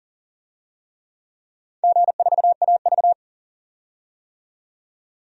Morse code, heard as a keyed tone, about 40 wpm, 700 Hz. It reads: G4AV
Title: DXCC-100 Call Signs - 4 Characters - 40wpm